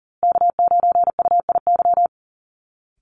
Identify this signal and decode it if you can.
Morse code, heard as a keyed tone, about 40 wpm, 700 Hz. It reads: K9UIY
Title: Call Signs Repeated 2x - 40wpm